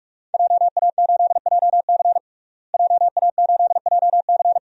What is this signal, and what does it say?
Morse code, heard as a keyed tone, about 46 wpm, 700 Hz. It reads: JA8JC JA8JC
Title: DXCC-100 Call Signs - 5 Characters - Repeated 2x - 45wpm